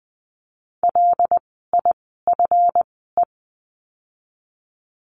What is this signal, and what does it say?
Morse code, heard as a keyed tone, about 20 wpm, spaced as for 15 wpm, 700 Hz. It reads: LIFE
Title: 139 Lesson 23f - Letter f - Words 20wpm